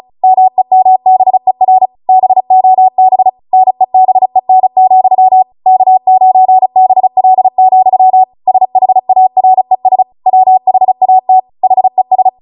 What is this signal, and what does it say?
Morse code, heard as a keyed tone, about 36 wpm, 750 Hz. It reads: MEMBER BOB NEBEN, K9BL, SHARES WHAT HES